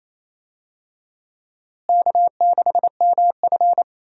Morse code, heard as a keyed tone, about 28 wpm, 700 Hz. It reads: K6MF
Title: USA Call Signs - 4 Characters - Repeated 3x - 28wpm